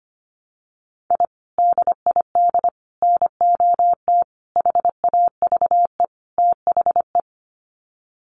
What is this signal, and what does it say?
Morse code, heard as a keyed tone, about 25 wpm, 700 Hz. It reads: I DID NOT HAVE THE
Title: Sets of 5 Words Encore 25wpm